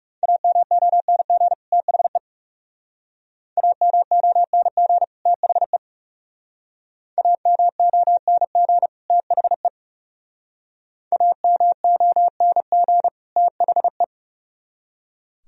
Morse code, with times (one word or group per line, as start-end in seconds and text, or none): 0.2-1.5 AMONG
1.7-2.2 THE
3.6-5.0 AMONG
5.3-5.8 THE
7.2-8.9 AMONG
9.1-9.7 THE
11.1-13.1 AMONG
13.4-14.0 THE